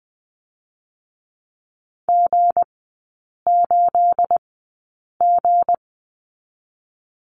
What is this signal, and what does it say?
Morse code, heard as a keyed tone, about 20 wpm, 700 Hz. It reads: G 8 G
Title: 212 Lesson 34e - Number 8 - Three Characters 20wpm